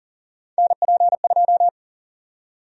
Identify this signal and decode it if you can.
Morse code, heard as a keyed tone, about 40 wpm, 700 Hz. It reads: NP2
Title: USA Call Sign Prefixes - 40wpm